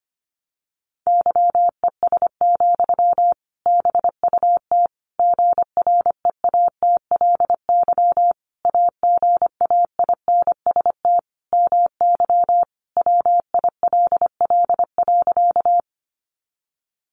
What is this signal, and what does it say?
Morse code, heard as a keyed tone, about 25 wpm, 700 Hz. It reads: YES, BUT GREATLY AGAINST MY WILL.
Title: Sentences from Top 1400 Words 25wpm